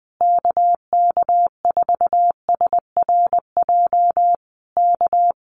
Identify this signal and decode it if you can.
Morse code, heard as a keyed tone, about 20 wpm, 700 Hz. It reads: KK4SRJ K
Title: Common QSO Phrases Encore 20wpm